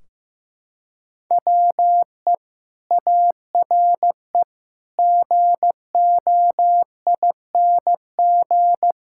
Morse code, heard as a keyed tone, about 15 wpm, 700 Hz. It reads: WE ARE GOING
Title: Sentences from Top 1600 Words 15wpm